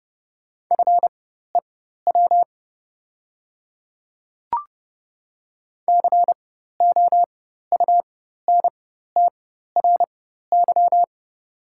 Morse code, heard as a keyed tone, about 30 wpm, spaced as for 14 wpm, 700 Hz. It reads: FEW COUNTRY
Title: 4x Character Spacing - Top 300 Words - 30wpm